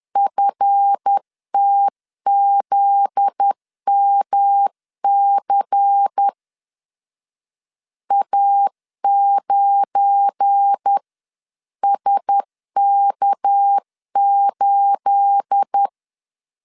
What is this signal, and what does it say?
Morse code, heard as a keyed tone, about 10 wpm, 800 Hz. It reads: FTZMC A9SK8
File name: LEZIONI / 9 28 10wpm 2842